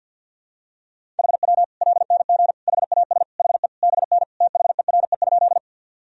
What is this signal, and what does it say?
Morse code, heard as a keyed tone, about 50 wpm, 700 Hz. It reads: HW LNG HAS HE BN THERE?
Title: Sentences from Top 100 Words and Abbreviations 50wpm